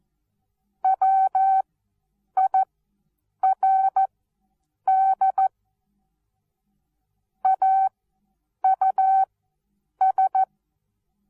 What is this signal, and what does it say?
Morse code, heard as a keyed tone, about 14 wpm, spaced as for 8 wpm, 770 Hz. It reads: WIRD AUS
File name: Onlinelehrgang / morsen / DJ4UF Morsen19